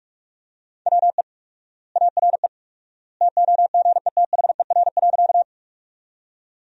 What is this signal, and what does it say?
Morse code, heard as a keyed tone, about 41 wpm, 700 Hz. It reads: WE ARE TOGETHER.
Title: 4x Word-Spacing - Sentences from Top 300 Words - 45wpm